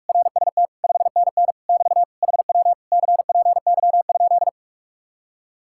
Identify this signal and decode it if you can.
Morse code, heard as a keyed tone, about 49 wpm, 700 Hz. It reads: RST 5NN <BT> HW CPY?